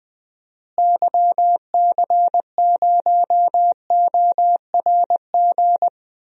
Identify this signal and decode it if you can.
Morse code, heard as a keyed tone, about 20 wpm, 700 Hz. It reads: YC0ORG
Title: DXCC-100 Call Signs - 6 Characters - Repeated 2x - 20wpm